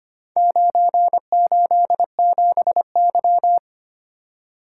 Morse code, heard as a keyed tone, about 25 wpm, 700 Hz. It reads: 987Y
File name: practice / 25wpm / Four Letter-Number Combinations - 25wpm